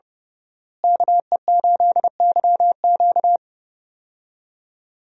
Morse code, heard as a keyed tone, about 30 wpm, 700 Hz. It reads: KE8YQ
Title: USA Call Signs - 5 Characters - Repeated 2x - 30wpm